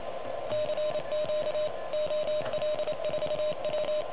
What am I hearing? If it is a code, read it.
Morse code, heard as a keyed tone, about 27 wpm, 600 Hz, with noise. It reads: CQ OL4W
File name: truSDX